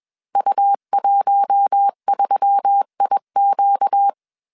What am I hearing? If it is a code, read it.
Morse code, heard as a keyed tone, about 21 wpm, 800 Hz. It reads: U13IQ
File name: CW / DET / 1 6 1500C S B 20wpm 2048